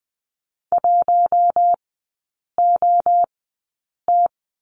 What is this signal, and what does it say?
Morse code, heard as a keyed tone, about 20 wpm, spaced as for 8 wpm, 700 Hz. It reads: 1OT
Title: 087 Lesson 15e - Number 5 - Three Characters 20wpm